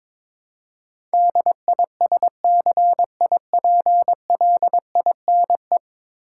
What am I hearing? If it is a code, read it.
Morse code, heard as a keyed tone, about 22 wpm, 700 Hz. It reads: DISCIPLINE
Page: morse-code-ninja-practice-22wpm